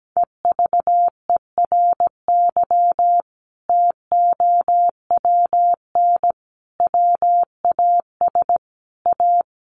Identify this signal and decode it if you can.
Morse code, heard as a keyed tone, about 17 wpm, 700 Hz. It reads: EVERY TOWN WAS A
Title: Sentences from Top 1500 Words 17wpm